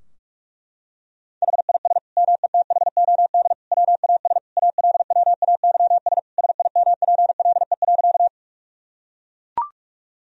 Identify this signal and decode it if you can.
Morse code, heard as a keyed tone, about 42 wpm, 700 Hz. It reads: HIS METHOD WAS ALWAYS SIMPLE.
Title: Sentences from Top 1000 Words - Review Part 3 - 45wpm